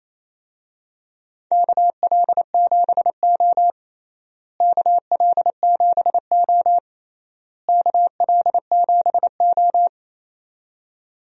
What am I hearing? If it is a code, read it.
Morse code, heard as a keyed tone, about 27 wpm, 700 Hz. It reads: KL7O KL7O KL7O
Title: DXCC-100 Call Signs - 4 Characters - Repeated 3x - 28wpm